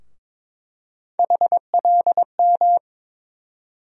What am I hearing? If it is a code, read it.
Morse code, heard as a keyed tone, about 22 wpm, 700 Hz. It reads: HLM
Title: USA Call Sign Postfixes - 3 Letters 22wpm